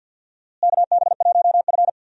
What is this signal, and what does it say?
Morse code, heard as a keyed tone, about 49 wpm, 700 Hz. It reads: KB1F
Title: USA Call Signs - 4 Characters - Repeated 3x - 50wpm